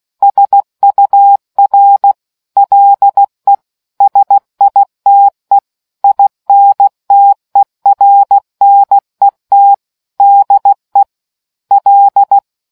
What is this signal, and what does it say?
Morse code, heard as a keyed tone, about 16 wpm, 800 Hz. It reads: SUR LE SITE INTERNET DE L